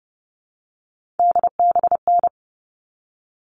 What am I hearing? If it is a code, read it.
Morse code, heard as a keyed tone, about 30 wpm, 700 Hz. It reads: DBN
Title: All Cut Numbers - 3-Digit 30wpm